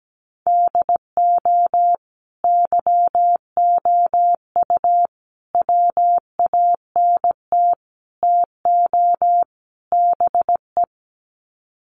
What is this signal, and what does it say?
Morse code, heard as a keyed tone, about 17 wpm, 700 Hz. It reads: DO YOU WANT TO BE
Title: Sets of 5 Words Encore2 17wpm